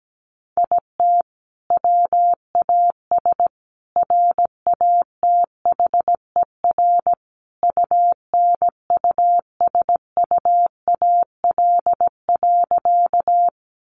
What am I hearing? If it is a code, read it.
Morse code, heard as a keyed tone, about 17 wpm, 700 Hz. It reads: IT WAS RATHER UNUSUAL.
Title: Sentences from Top 1400 Words 17wpm